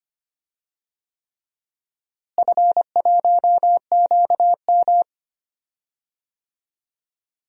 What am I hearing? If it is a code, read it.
Morse code, heard as a keyed tone, about 25 wpm, 700 Hz. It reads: F1QM
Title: DXCC-100 Call Signs - 4 Characters - Repeated 3x - 25wpm